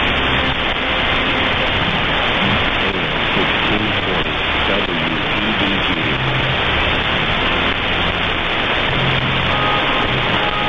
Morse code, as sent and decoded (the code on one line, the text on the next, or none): none